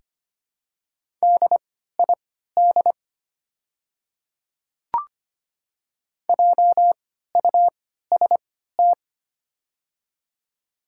Morse code, heard as a keyed tone, about 25 wpm, spaced as for 14 wpm, 700 Hz. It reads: DID JUST
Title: 3x Character Spacing - Top 200 Words - 25wpm